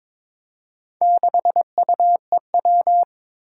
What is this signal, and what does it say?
Morse code, heard as a keyed tone, about 22 wpm, 700 Hz. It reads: 6UEW
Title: Four Challenging Letter-Number Combinations 22wpm